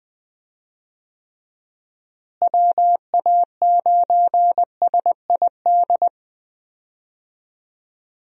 Morse code, heard as a keyed tone, about 20 wpm, 700 Hz. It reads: WA9SID
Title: USA Call Signs - 6 Characters - Repeated 2x - 20wpm